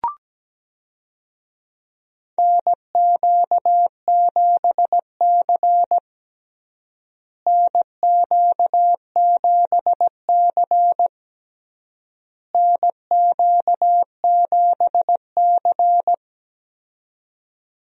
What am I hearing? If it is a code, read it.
Morse code, heard as a keyed tone, about 17 wpm, 700 Hz. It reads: NQ7C NQ7C NQ7C
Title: USA Call Signs - 4 Characters - Repeated 3x - 17wpm